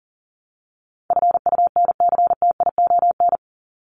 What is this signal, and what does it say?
Morse code, heard as a keyed tone, about 40 wpm, 700 Hz. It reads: FUNCTION
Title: ES:S Suffix - Top 100 Words with Base Words 40wpm